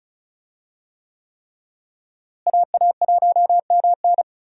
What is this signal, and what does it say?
Morse code, heard as a keyed tone, about 36 wpm, 700 Hz. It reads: AA1MN
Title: USA Call Signs - 5 Characters - Repeated 3x - 35wpm